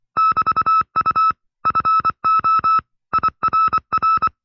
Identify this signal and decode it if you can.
Morse code, heard as a keyed tone, about 25 wpm, 1310 Hz. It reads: <BT>U FO IRR
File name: FreeFiles / CW-practice / 20WPM-side2